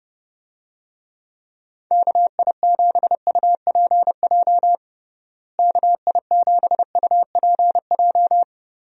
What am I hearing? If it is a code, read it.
Morse code, heard as a keyed tone, about 30 wpm, 700 Hz. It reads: KI7UPJ KI7UPJ
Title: USA Call Signs - 6 Characters - Repeated 2x - 30wpm